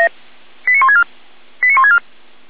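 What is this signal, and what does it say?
Five-tone selective call (ZVEI), heard as quick runs of five tones, tones of about 80 ms. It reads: 89164 89164
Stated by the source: ZVEI five-tone fire brigade selective call received on an SDR by Feuerwehr Orlamünde.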